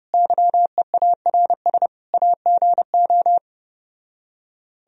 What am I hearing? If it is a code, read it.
Morse code, heard as a keyed tone, about 33 wpm, 700 Hz. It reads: YEARS AGO